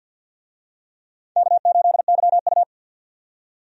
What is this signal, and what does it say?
Morse code, heard as a keyed tone, about 53 wpm, 700 Hz. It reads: K8YU
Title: USA Call Signs - 4 Characters - 50wpm